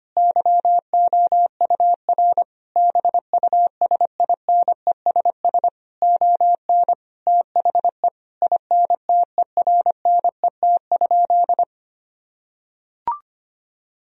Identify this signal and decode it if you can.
Morse code, heard as a keyed tone, about 25 wpm, 700 Hz. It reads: YOUR BUSINESS ON THE INTERNET?
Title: Sentences from Top 1000 Words - Review Part 4 - 25wpm